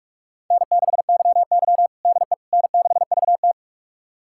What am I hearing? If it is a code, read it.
Morse code, heard as a keyed tone, about 46 wpm, 700 Hz. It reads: N6YY DE N6VT
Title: USA Callsign DE Callsign - 4 Characters - 45wpm